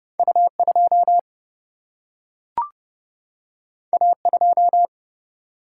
Morse code, heard as a keyed tone, about 30 wpm, 700 Hz. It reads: U 2 A 2
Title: Single-Letter Single-Number 30wpm